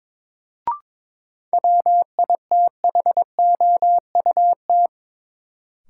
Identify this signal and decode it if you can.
Morse code, heard as a keyed tone, about 21 wpm, 700 Hz. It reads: WITHOUT
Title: Sentences from Top 600 Words - New Word Drill - 22wpm